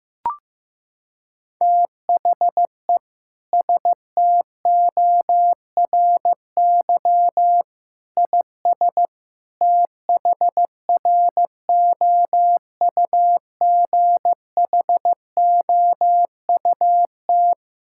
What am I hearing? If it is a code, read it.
Morse code, heard as a keyed tone, about 16 wpm, 700 Hz. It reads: THE STORY IS THROUGHOUT
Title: Sentences from Top 1000 Words - Review Part 5 - 15wpm